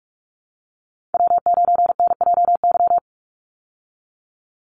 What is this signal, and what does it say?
Morse code, heard as a keyed tone, about 44 wpm, 700 Hz. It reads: W9NJY
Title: Call Signs Repeated 3x - 45wpm